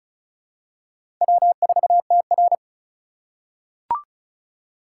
Morse code, heard as a keyed tone, about 34 wpm, 700 Hz. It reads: W4TR
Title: USA Call Signs - 4 Characters - 2x Character Spacing - 35wpm